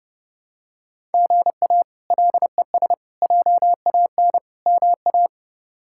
Morse code, heard as a keyed tone, about 29 wpm, 700 Hz. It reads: GA LES JAN MA
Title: K1USN SST QSO Practice - 30wpm